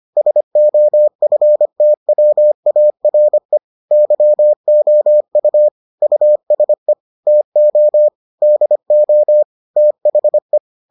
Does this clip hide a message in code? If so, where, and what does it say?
Yes: Morse code from 0.2 to 10.6 s, heard as a keyed tone, about 25 wpm, 600 Hz. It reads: SOFTWARE YOU USE TO DO THE